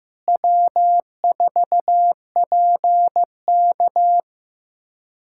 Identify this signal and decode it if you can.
Morse code, heard as a keyed tone, about 15 wpm, 700 Hz. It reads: W4PK
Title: K1USN SST QSO Practice - 15wpm